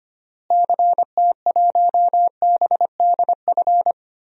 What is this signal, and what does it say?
Morse code, heard as a keyed tone, about 25 wpm, 700 Hz. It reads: CT1BDF
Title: DXCC-100 Call Signs - 6 Characters - Repeated 3x - 25wpm